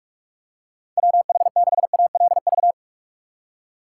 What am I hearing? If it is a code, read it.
Morse code, heard as a keyed tone, about 46 wpm, 700 Hz. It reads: WH6ALV